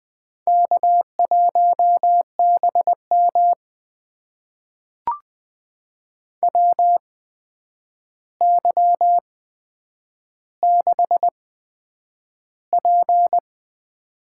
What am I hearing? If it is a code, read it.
Morse code, heard as a keyed tone, about 20 wpm, spaced as for 9 wpm, 700 Hz. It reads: K1BM WY6P
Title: USA Call Signs - 8x Character Spacing - 20wpm